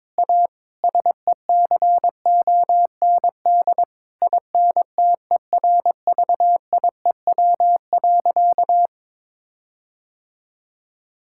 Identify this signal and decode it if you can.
Morse code, heard as a keyed tone, about 22 wpm, 700 Hz. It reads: A SECOND INTERVIEW.
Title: Sentences from Top 1600 Words 22wpm